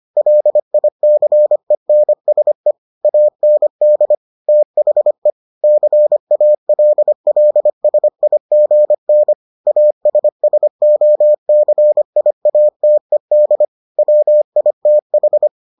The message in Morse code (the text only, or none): LICENSE AND THE CALLSIGN ASSOCIATED WITH